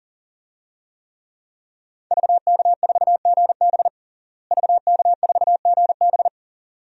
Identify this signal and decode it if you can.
Morse code, heard as a keyed tone, about 40 wpm, 700 Hz. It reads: VK4GB VK4GB
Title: DXCC-100 Call Signs - 5 Characters - Repeated 2x - 40wpm